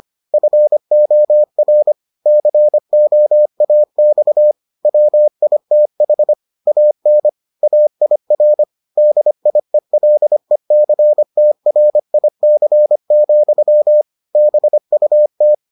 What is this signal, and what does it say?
Morse code, heard as a keyed tone, about 25 wpm, 600 Hz. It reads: FOR COAX WITH AN AIR DIELECTRIC, BUT